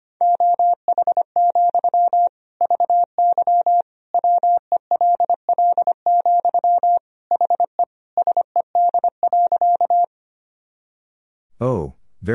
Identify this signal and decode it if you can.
Morse code, heard as a keyed tone, about 25 wpm, 700 Hz. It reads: OH, VY WELL, HE SED.
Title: Sentences from Top 500 Words and Abbreviations 25wpm